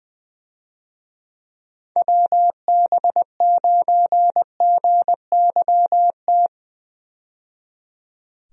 Morse code, heard as a keyed tone, about 21 wpm, 700 Hz. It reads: WB9GYT
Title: USA Call Signs - 6 Characters - 20wpm